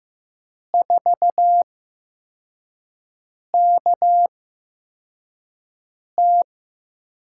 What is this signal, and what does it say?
Morse code, heard as a keyed tone, about 15 wpm, spaced as for 4 wpm, 700 Hz. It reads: 4KT
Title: USA Call Signs - 4 Characters - 8x Character Spacing - 15wpm